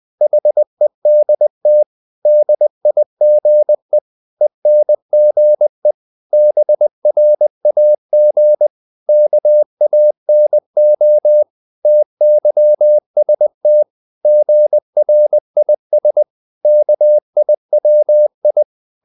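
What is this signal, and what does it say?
Morse code, heard as a keyed tone, about 20 wpm, 600 Hz. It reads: HEDT DIGE ENGE BRAG KANO TYST GRIS KIWI
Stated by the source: Morse code practice of short Danish words at 20 wpm, from CW med Gnister.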